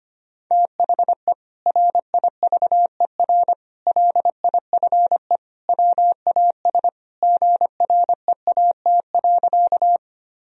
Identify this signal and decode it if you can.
Morse code, heard as a keyed tone, about 25 wpm, 700 Hz. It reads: THE RIVER LIFE WAS GREAT.
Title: Sentences from Top 1000 Words - Review Part 1 - 25wpm